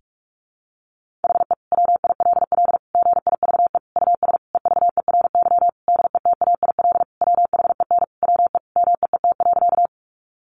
Morse code, heard as a keyed tone, about 44 wpm, 700 Hz. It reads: HE WILL GIVE US EVERY DETAIL WHEN WE MEET.
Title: Sentences from Top 1500 Words 45wpm